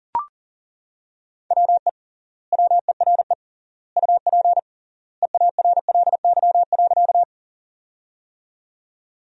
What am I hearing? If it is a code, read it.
Morse code, heard as a keyed tone, about 40 wpm, 700 Hz. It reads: WE WERE UP EARLY.
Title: 3x Word-Spacing - Sentences from Top 400 Words - 40wpm